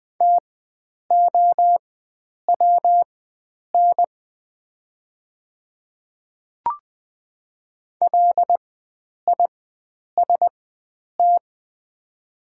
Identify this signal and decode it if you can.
Morse code, heard as a keyed tone, about 20 wpm, spaced as for 9 wpm, 700 Hz. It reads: TOWN LIST
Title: 4x Character Spacing - Top 500 Words - 20wpm